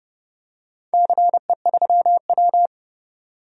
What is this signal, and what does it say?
Morse code, heard as a keyed tone, about 32 wpm, 700 Hz. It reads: CE3W
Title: DXCC-100 Call Signs - 4 Characters - 30wpm